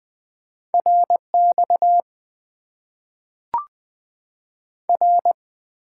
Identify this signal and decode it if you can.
Morse code, heard as a keyed tone, about 20 wpm, 700 Hz. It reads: RX R
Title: Top 90 QSO Elements - 8x Character Spacing - 20wpm